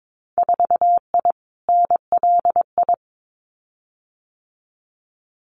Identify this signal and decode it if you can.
Morse code, heard as a keyed tone, about 22 wpm, 700 Hz. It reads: 4I NLI
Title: Winter Field Day Exchanges Encore 22wpm